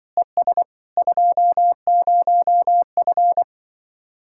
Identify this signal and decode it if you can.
Morse code, heard as a keyed tone, about 24 wpm, 700 Hz. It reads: ES 20F
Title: Speed-Racing Common QSO Phrases - 20wpm